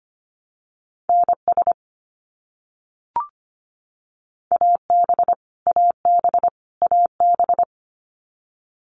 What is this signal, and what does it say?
Morse code, heard as a keyed tone, about 25 wpm, 700 Hz. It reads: NS AB AB AB
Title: Canadian Provinces and Territories - Repeated 3x - 25wpm